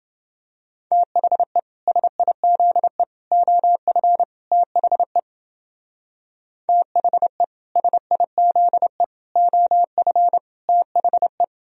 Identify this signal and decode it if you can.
Morse code, heard as a keyed tone, about 29 wpm, 700 Hz. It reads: THE SIZE OF THE THE SIZE OF THE